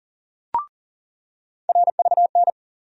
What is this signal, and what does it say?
Morse code, heard as a keyed tone, about 40 wpm, 700 Hz. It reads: RVN
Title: USA Call Sign Postfixes - 3 Letters 40wpm